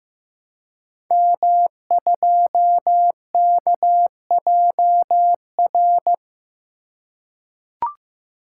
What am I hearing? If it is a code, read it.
Morse code, heard as a keyed tone, about 15 wpm, 700 Hz. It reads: M2KJR
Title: Code Groups - 5 Letter:Numbers - 15wpm